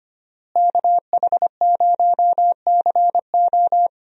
Morse code, heard as a keyed tone, about 25 wpm, 700 Hz. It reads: KH0CO
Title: USA Call Signs - 5 Characters - Repeated 2x - 25wpm